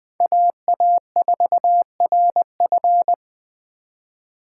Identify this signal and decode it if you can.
Morse code, heard as a keyed tone, about 20 wpm, 700 Hz. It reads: AA4RF
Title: 2023 POTA Activator Call Signs - 20wpm